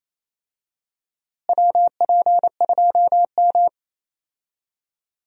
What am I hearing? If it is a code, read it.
Morse code, heard as a keyed tone, about 28 wpm, 700 Hz. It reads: WP2M
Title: USA Call Signs - 4 Characters - Repeated 3x - 28wpm